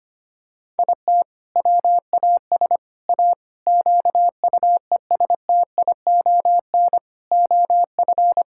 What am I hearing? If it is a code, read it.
Morse code, heard as a keyed tone, about 25 wpm, 700 Hz. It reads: IT WAS A QUESTION OF
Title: Sentences from Top 1000 Words - Review Part 6 - 25wpm